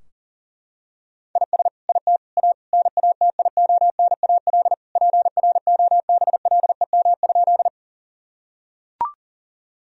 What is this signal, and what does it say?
Morse code, heard as a keyed tone, about 37 wpm, 700 Hz. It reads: IS IT A NATIONAL PROBLEM?
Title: Sentences from Top 1000 Words - Review Part 4 - 40wpm